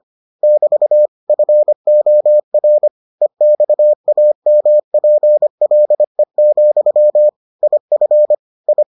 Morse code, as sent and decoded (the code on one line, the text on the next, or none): -...- / ..-. --- .-. / . -..- .- -- .--. .-.. --..-- / .. ..-. / ..
<BT> FOR EXAMPLE, IF I